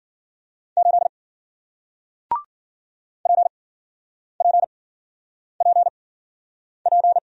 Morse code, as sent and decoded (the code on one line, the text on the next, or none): ---.. / .--. .--. .--. .--.
8 PPPP